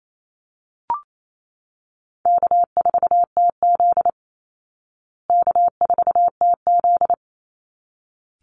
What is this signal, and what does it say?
Morse code, heard as a keyed tone, about 28 wpm, 700 Hz. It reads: K4TZ K4TZ
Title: Easy Call Signs Repeated 2x - 28wpm